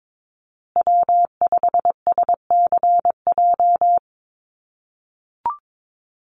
Morse code, heard as a keyed tone, about 22 wpm, 700 Hz. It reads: W5SCJ
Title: USA Call Signs Repeated 2x 22wpm